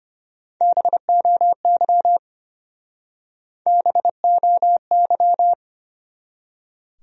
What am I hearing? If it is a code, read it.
Morse code, heard as a keyed tone, about 30 wpm, 700 Hz. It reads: BOY BOY
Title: Speed-Racing Top 200 Words 25wpm